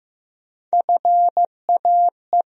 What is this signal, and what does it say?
Morse code, heard as a keyed tone, about 15 wpm, 700 Hz. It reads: FAE
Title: USA Call Sign Postfixes - 3 Letters 15wpm